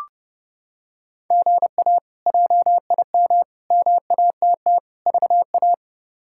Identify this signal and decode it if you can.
Morse code, heard as a keyed tone, about 30 wpm, 700 Hz. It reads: GA JIM MATT VA
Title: K1USN SST QSO Practice - 30wpm